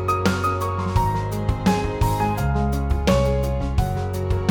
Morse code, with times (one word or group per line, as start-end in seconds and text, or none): none